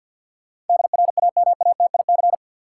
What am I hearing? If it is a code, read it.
Morse code, heard as a keyed tone, about 50 wpm, 700 Hz. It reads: DRAMATIC
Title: Sentences from Top 1300 Words - New Word Drill - 50wpm